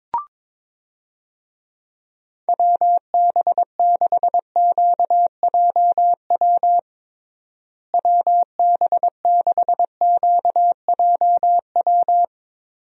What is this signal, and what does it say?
Morse code, heard as a keyed tone, about 22 wpm, 700 Hz. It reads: WB6QJW WB6QJW
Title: USA Call Signs - 6 Characters - Repeated 3x - 22wpm